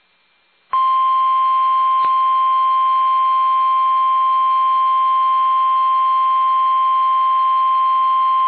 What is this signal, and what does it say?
Emergency Alert System (SAME) tones (the 1050 Hz attention tone). It is heard from 0.7 to 8.2 s.
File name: eas / KUSO / KUSO